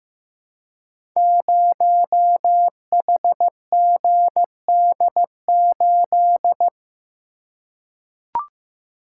Morse code, heard as a keyed tone, about 15 wpm, 700 Hz. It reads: 0HGD8
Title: Code Groups - 5 Letter:Numbers - 15wpm